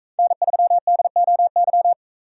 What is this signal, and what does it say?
Morse code, heard as a keyed tone, about 40 wpm, 700 Hz. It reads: N3DOY